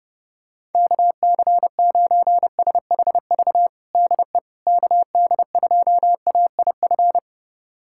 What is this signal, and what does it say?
Morse code, heard as a keyed tone, about 30 wpm, 700 Hz. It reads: KC9SHV DE KD2AIF
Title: USA Callsign DE Callsign - 6 Characters - 30wpm